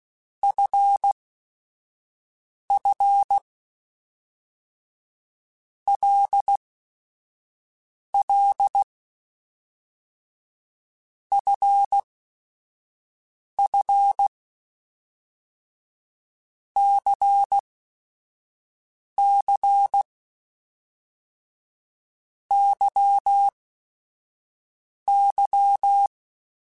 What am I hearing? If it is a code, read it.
Morse code, heard as a keyed tone, about 16 wpm, spaced as for 5 wpm, 800 Hz. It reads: FF LL FF CC YY
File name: Basic Sets / Basic Letter Groups Collective7 / LFCYQ@5